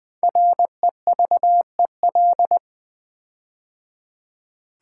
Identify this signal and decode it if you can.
Morse code, heard as a keyed tone, about 20 wpm, 700 Hz. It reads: REVEL
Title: Build A Word - 5 Letters At A Time - 10-Letter Words - 20wpm